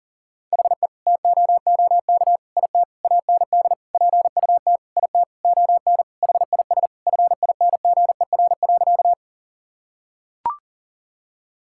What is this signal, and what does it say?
Morse code, heard as a keyed tone, about 40 wpm, 700 Hz. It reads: HE TOOK IT AND PUT IT ON HIS FINGER.
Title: Sentences from Top 1600 Words 40wpm